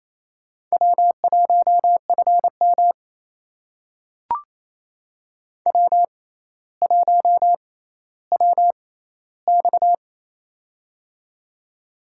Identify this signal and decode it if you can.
Morse code, heard as a keyed tone, about 28 wpm, spaced as for 14 wpm, 700 Hz. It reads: W1FM W1WX
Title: USA Call Signs - 6x Character Spacing - 28wpm